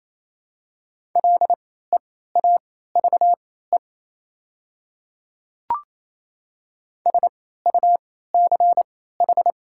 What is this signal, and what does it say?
Morse code, heard as a keyed tone, about 28 wpm, spaced as for 16 wpm, 700 Hz. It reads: LEAVE SUCH
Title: 3x Character Spacing - Top 500 Words - 28wpm